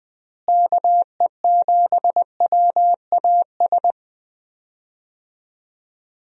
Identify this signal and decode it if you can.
Morse code, heard as a keyed tone, about 20 wpm, 700 Hz. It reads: KE7WAS
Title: USA Call Signs - 6 Characters - Repeated 3x - 20wpm